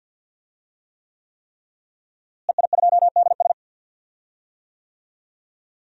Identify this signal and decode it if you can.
Morse code, heard as a keyed tone, about 52 wpm, 700 Hz. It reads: EI2DS
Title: DXCC-100 Call Signs - 5 Characters - Repeated 3x - 50wpm